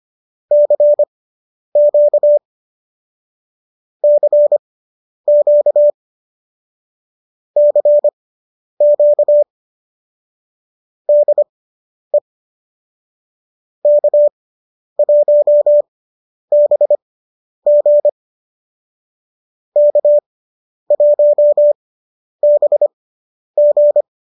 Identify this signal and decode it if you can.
Morse code, heard as a keyed tone, about 25 wpm, spaced as for 10 wpm, 590 Hz. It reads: CQ CQ CQ DE K1BG K1BG